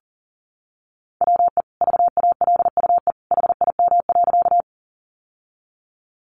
Morse code, heard as a keyed tone, about 40 wpm, 700 Hz. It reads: WE VALUE HIM.
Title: Sentences from Top 600 Words 40wpm